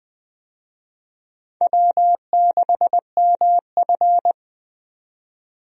Morse code, heard as a keyed tone, about 20 wpm, 700 Hz. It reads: W6MF
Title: USA Call Signs - 4 Characters - 20wpm